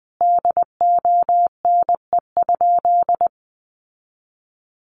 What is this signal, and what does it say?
Morse code, heard as a keyed tone, about 20 wpm, 700 Hz. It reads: DONE?
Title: Sentences from Top 1100 Words 20wpm